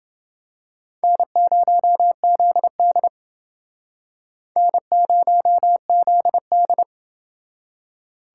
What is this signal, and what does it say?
Morse code, heard as a keyed tone, about 29 wpm, 700 Hz. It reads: N0ZD N0ZD